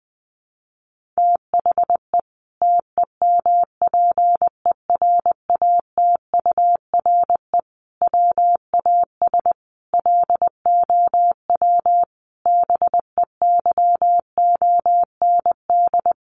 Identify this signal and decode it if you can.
Morse code, heard as a keyed tone, about 20 wpm, 700 Hz. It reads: THE TEMPERATURE WAS LOW BEYOND
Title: Sentences from Top 1500 Words 20wpm